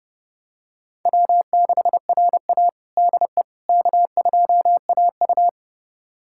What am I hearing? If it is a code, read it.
Morse code, heard as a keyed tone, about 30 wpm, 700 Hz. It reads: W6RA DE K2AU
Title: USA Callsign DE Callsign - 4 Characters - 30wpm